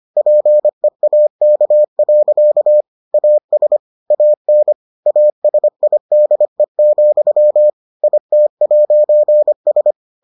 Morse code, heard as a keyed tone, about 25 wpm, 600 Hz. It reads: PEAK. AS AN ASIDE, IT'S